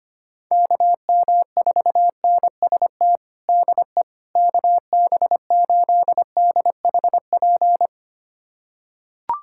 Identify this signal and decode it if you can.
Morse code, heard as a keyed tone, about 25 wpm, 700 Hz. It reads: KM4NST DE KB8DHP
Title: USA Callsign DE Callsign 25wpm